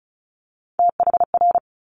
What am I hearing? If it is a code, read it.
Morse code, heard as a keyed tone, about 36 wpm, 700 Hz. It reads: THR
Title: Multi-letter Phonemes Quick Fire 35wpm